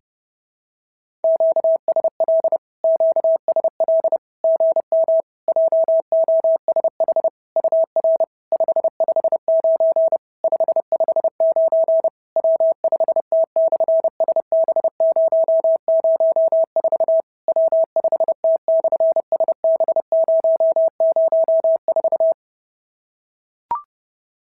Morse code, heard as a keyed tone, about 30 wpm, 650 Hz. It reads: QSL QSL GM JOSH UR 559 559 W5T/SB004 W5T/SB004
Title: SOTA QSO Practice 30wpm